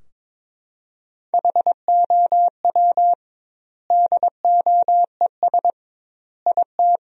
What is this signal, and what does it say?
Morse code, heard as a keyed tone, about 22 wpm, 700 Hz. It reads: HOW DOES IT
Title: 2x Word-Spacing - Sentences from Top 400 Words - 22wpm